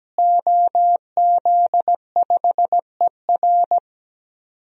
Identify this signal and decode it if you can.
Morse code, heard as a keyed tone, about 17 wpm, 700 Hz. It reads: OZ5ER
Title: DXCC-100 Call Signs - 5 Characters - Repeated 3x - 17wpm